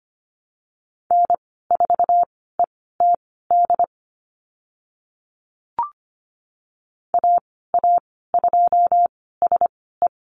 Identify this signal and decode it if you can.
Morse code, heard as a keyed tone, about 25 wpm, spaced as for 16 wpm, 700 Hz. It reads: N4ETD AA2SE
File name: Lesson 15g - Number 5 - Call Signs 25wpm